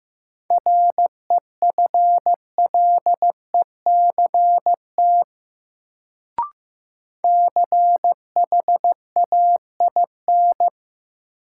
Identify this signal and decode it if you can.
Morse code, heard as a keyed tone, about 15 wpm, 700 Hz. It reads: REFLECT CHAIN
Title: Sentences from Top 1500 Words - New Word Drill Alt - 15wpm